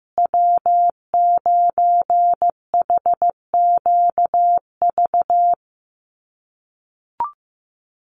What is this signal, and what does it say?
Morse code, heard as a keyed tone, about 15 wpm, 700 Hz. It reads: W9HQV
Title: USA Call Signs Repeated 2x 15wpm